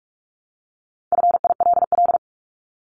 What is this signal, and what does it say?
Morse code, heard as a keyed tone, about 44 wpm, 700 Hz. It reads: FILL
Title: Top 500 Words Repeated 45wpm